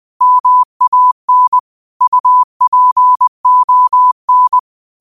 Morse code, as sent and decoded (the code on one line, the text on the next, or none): -- .- -. / ..- .--. --- -.
MAN UPON